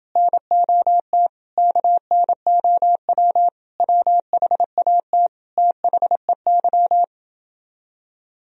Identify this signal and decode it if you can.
Morse code, heard as a keyed tone, about 31 wpm, 700 Hz. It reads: NOT KNOW WHAT THEY